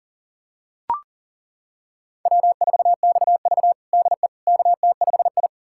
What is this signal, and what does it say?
Morse code, heard as a keyed tone, about 40 wpm, 700 Hz. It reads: W4XV DE KT5I
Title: USA Callsign DE Callsign - 4 Characters - 40wpm